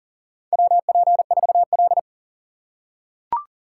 Morse code, heard as a keyed tone, about 40 wpm, 700 Hz. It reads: WP4L
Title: USA Call Signs - 4x Character Spacing - 40wpm